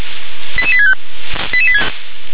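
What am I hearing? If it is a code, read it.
Five-tone selective call (ZVEI), heard as quick runs of five tones, tones of about 70 ms. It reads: 88865
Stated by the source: Five-tone ZVEI selective call received on an SDR.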